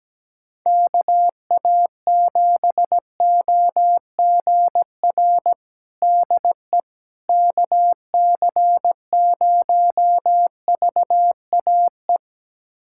Morse code, heard as a keyed tone, about 17 wpm, 700 Hz. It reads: KA7OGR DE KC0VAE
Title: USA Callsign DE Callsign - 6 Characters - 17wpm